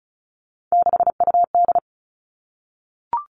Morse code, heard as a keyed tone, about 34 wpm, 700 Hz. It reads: 6UD
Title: All Cut Numbers - 3-Digit 35wpm